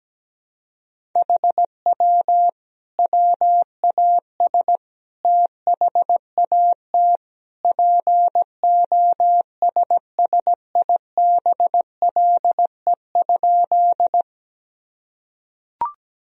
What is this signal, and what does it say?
Morse code, heard as a keyed tone, about 17 wpm, 700 Hz. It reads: HW WAS THAT POSSIBLE?
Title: Sentences from Top 500 Words and Abbreviations 17wpm